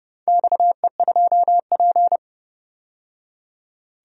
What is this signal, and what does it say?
Morse code, heard as a keyed tone, about 32 wpm, 700 Hz. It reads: XE2P
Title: DXCC-100 Call Signs - 4 Characters - 30wpm